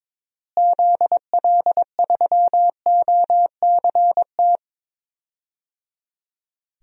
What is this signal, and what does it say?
Morse code, heard as a keyed tone, about 22 wpm, 700 Hz. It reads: ZL3OCT
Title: DXCC-100 Call Signs - 6 Characters - Repeated 3x - 22wpm